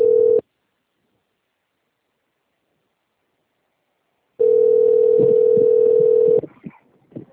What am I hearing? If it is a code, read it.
Ringback tone (North America), 440 + 480 Hz.